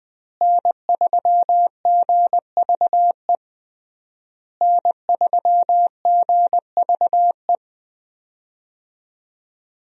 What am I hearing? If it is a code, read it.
Morse code, heard as a keyed tone, about 20 wpm, 700 Hz. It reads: N3GVE N3GVE
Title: USA Call Signs - 5 Characters - Repeated 3x - 20wpm